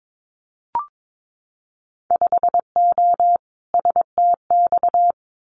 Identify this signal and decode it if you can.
Morse code, heard as a keyed tone, about 22 wpm, 700 Hz. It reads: 5O STX
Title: Winter Field Day Exchanges 22wpm